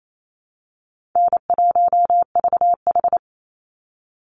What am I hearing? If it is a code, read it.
Morse code, heard as a keyed tone, about 28 wpm, 700 Hz. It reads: N1VH
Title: Easy Call Signs Repeated 3x - 28wpm